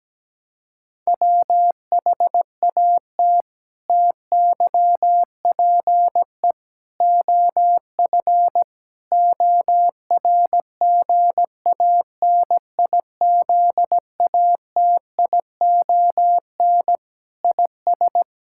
Morse code, heard as a keyed tone, about 17 wpm, 700 Hz. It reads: WHAT TYPE OF ORGANIZATION IS
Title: Sentences from Top 1000 Words - Review Part 5 - 17wpm